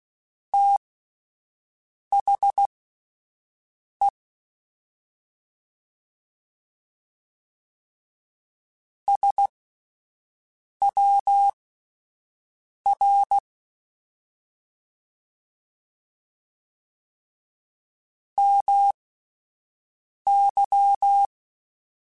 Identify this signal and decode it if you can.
Morse code, heard as a keyed tone, about 16 wpm, spaced as for 5 wpm, 800 Hz. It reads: THE SWR MY
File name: (spaced) / QSOs Spaced-mp3s / WD0FBT 1 @10 sp